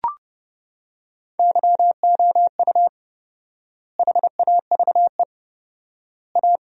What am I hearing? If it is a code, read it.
Morse code, heard as a keyed tone, about 30 wpm, 700 Hz. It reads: YOU HAVE A
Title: 4x Word-Spacing - Sentences from Top 300 Words - 30wpm